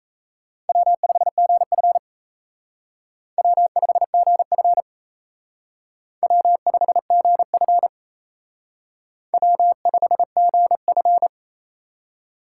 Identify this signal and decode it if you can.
Morse code, heard as a keyed tone, about 35 wpm, 700 Hz. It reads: W5GF W5GF W5GF W5GF